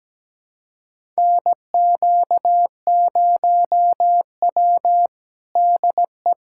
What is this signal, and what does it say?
Morse code, heard as a keyed tone, about 17 wpm, 700 Hz. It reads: NQ0W DE
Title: USA Callsign DE Callsign 17wpm